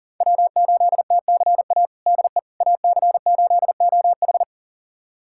Morse code, heard as a keyed tone, about 40 wpm, 700 Hz. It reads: W8TCA DE AC8OH
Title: USA Callsign DE Callsign - 5 Characters - 40wpm